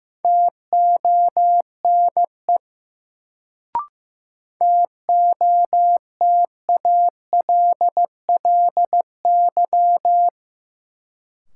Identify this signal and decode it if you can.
Morse code, heard as a keyed tone, about 15 wpm, 700 Hz. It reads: TONE TOTALLY
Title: Sentences from Top 1600 Words - New Word Drill - 15wpm